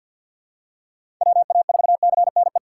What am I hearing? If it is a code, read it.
Morse code, heard as a keyed tone, about 52 wpm, 700 Hz. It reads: WA4CNE
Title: USA Call Signs - 4x Character Spacing - 50wpm